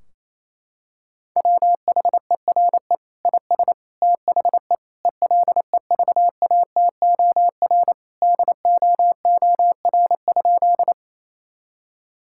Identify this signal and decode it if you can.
Morse code, heard as a keyed tone, about 28 wpm, 700 Hz. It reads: WHERE IS THE ELEVATOR DOOR?
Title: Sentences from Top 1600 Words 28wpm